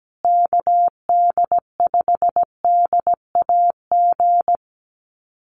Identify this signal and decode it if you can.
Morse code, heard as a keyed tone, about 17 wpm, 700 Hz. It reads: KD5DAG
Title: USA Call Signs Repeated 2x 17wpm